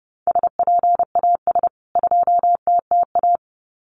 Morse code, heard as a keyed tone, about 30 wpm, 700 Hz. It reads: SPAS 2TTA
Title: CWT Exchange 30wpm